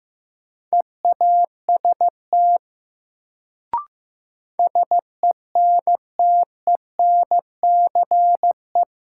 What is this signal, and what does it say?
Morse code, heard as a keyed tone, about 16 wpm, 700 Hz. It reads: EAST SENTENCE
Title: Sentences from Top 1200 Words - New Word Drill Alt - 15wpm